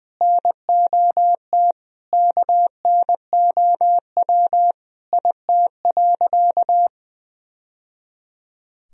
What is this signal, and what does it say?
Morse code, heard as a keyed tone, about 20 wpm, 700 Hz. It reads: NOT KNOW IT.
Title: Sentences from Top 1000 Words - Review Part 1 - 20wpm